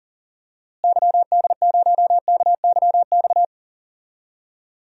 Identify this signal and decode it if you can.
Morse code, heard as a keyed tone, about 40 wpm, 700 Hz. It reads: YD0KYX
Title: DXCC-100 Call Signs - 6 Characters - Repeated 2x - 40wpm